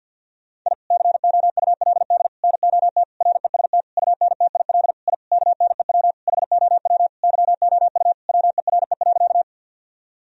Morse code, heard as a keyed tone, about 51 wpm, 700 Hz. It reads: I COULD NOT REST UNTIL I KNEW HOW YOU WERE.
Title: Sentences from Top 1000 Words - Review Part 1 - 50wpm